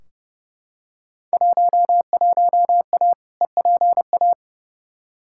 Morse code, heard as a keyed tone, about 32 wpm, 700 Hz. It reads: 11A EPA
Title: Field Day Exchanges Encore - 30wpm